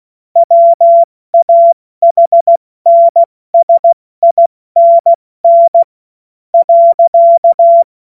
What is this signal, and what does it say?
Morse code, heard as a keyed tone, about 16 wpm, spaced as for 14 wpm, 680 Hz. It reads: WAHNSINN .
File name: Morsekurs / Witze / Witz 21. Februar 2023 70 F80